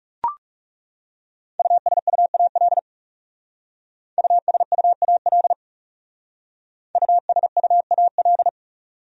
Morse code, heard as a keyed tone, about 38 wpm, 700 Hz. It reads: USUAL USUAL USUAL